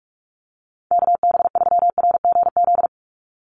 Morse code, heard as a keyed tone, about 44 wpm, 700 Hz. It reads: KB3RGZ
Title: USA Call Signs 45wpm